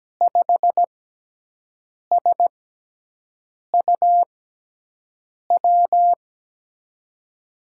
Morse code, heard as a keyed tone, about 17 wpm, spaced as for 6 wpm, 700 Hz. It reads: 5SUW
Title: USA Call Signs - 6x Character Spacing - 17wpm